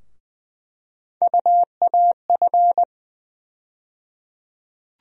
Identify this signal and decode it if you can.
Morse code, heard as a keyed tone, about 20 wpm, 700 Hz. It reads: UAF
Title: USA Call Sign Postfixes - 3 Letters 20wpm